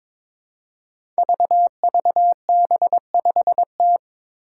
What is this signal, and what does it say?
Morse code, heard as a keyed tone, about 22 wpm, 700 Hz. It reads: VVB5T
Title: Five Challenging Letter-Number Combinations 22wpm